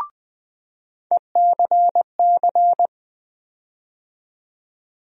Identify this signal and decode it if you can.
Morse code, heard as a keyed tone, about 20 wpm, 700 Hz. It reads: ECC
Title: USA Call Sign Postfixes - 3 Letters 20wpm